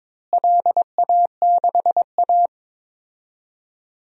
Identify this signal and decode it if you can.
Morse code, heard as a keyed tone, about 22 wpm, 700 Hz. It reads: LA6A
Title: DXCC-100 Call Signs - 4 Characters - Repeated 3x - 22wpm